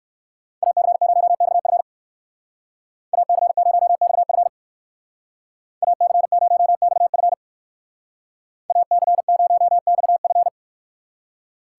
Morse code, heard as a keyed tone, about 59 wpm, 700 Hz. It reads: AC0XF AC0XF AC0XF AC0XF